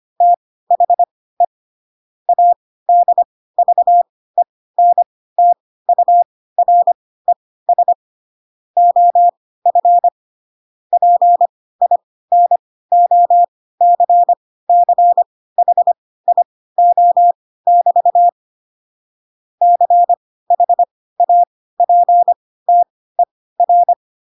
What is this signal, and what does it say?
Morse code, heard as a keyed tone, about 25 wpm, spaced as for 16 wpm, 700 Hz. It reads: THE ADVENTURES OF PINOCCHIO<BT> CHAPTER
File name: Pinocchio / Pinocchio ch1 10 25 ik1hge